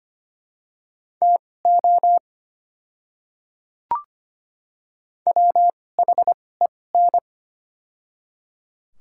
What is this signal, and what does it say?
Morse code, heard as a keyed tone, about 25 wpm, spaced as for 18 wpm, 700 Hz. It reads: TO WHEN
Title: 2x Character Spacing - Top 100 Words - 25wpm